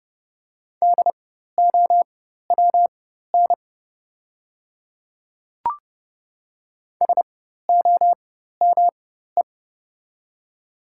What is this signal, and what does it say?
Morse code, heard as a keyed tone, about 30 wpm, spaced as for 14 wpm, 700 Hz. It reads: DOWN SOME
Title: 4x Character Spacing - Top 100 Words - 30wpm